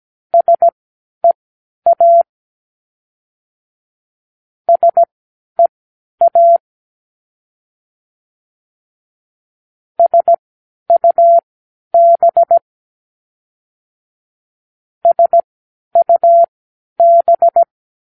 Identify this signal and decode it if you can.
Morse code, heard as a keyed tone, about 17 wpm, spaced as for 10 wpm, 680 Hz. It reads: SEA SEA SUB SUB
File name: k7qo / Practicewords